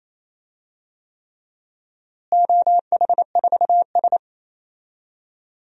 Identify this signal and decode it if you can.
Morse code, heard as a keyed tone, about 27 wpm, 700 Hz. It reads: OH4S
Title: DXCC-100 Call Signs - 4 Characters - Repeated 3x - 28wpm